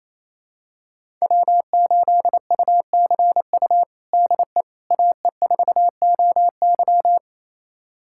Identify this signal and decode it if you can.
Morse code, heard as a keyed tone, about 28 wpm, 700 Hz. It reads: W8UCU DE AE4OY
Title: USA Callsign DE Callsign 28wpm